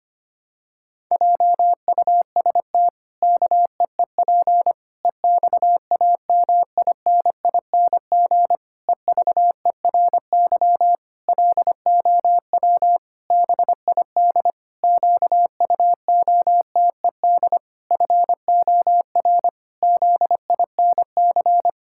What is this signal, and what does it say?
Morse code, heard as a keyed tone, about 25 wpm, 700 Hz. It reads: JUST KEEP EXAMINING EVERY LOW BID QUOTED FOR ZINC
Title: Pangram Challenge v2 - 25wpm